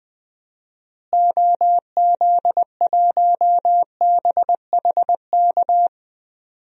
Morse code, heard as a keyed tone, about 20 wpm, 700 Hz. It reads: OZ1BHK
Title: DXCC-100 Call Signs - 6 Characters - Repeated 3x - 20wpm